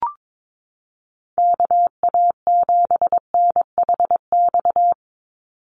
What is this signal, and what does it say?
Morse code, heard as a keyed tone, about 22 wpm, 700 Hz. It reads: KA7NHX
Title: USA Call Signs 22wpm